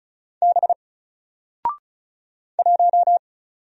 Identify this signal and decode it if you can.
Morse code, heard as a keyed tone, about 36 wpm, 700 Hz. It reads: B 1
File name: Single-Letter-Number-35wpm